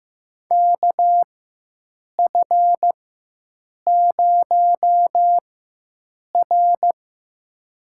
Morse code, heard as a keyed tone, about 15 wpm, spaced as for 7 wpm, 700 Hz. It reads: KF0R
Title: USA Call Signs - 4 Characters - 4x Character Spacing - 15wpm